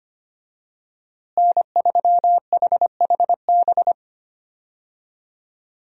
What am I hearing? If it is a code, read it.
Morse code, heard as a keyed tone, about 25 wpm, 700 Hz. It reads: N3HHB
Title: USA Call Signs - 5 Characters - Repeated 2x - 25wpm